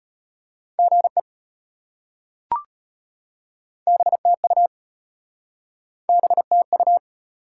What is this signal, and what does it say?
Morse code, heard as a keyed tone, about 33 wpm, 700 Hz. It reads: GE BTU BTU